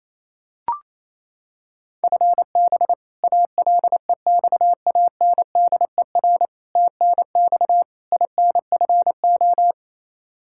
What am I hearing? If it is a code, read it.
Morse code, heard as a keyed tone, about 28 wpm, 700 Hz. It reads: FB ALEXANDER TNX INFO
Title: K1USN SST QSO Practice 28wpm